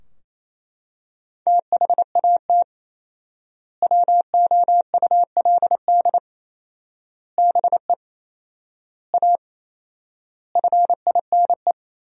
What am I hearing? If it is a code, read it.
Morse code, heard as a keyed tone, about 28 wpm, 700 Hz. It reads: THAT WOULD BE A FINE
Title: 4x Word-Spacing - Sentences from Top 500 Words - 28wpm